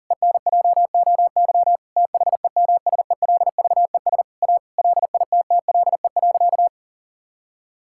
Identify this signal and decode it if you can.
Morse code, heard as a keyed tone, about 40 wpm, 700 Hz. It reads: ENJOY THEMSELVES A LITTLE.
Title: Sentences from Top 1000 Words - Review Part 5 - 40wpm